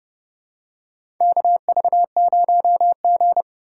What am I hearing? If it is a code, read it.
Morse code, heard as a keyed tone, about 32 wpm, 700 Hz. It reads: KV0G
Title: USA Call Signs - 4 Characters - 3x Character Spacing - 30wpm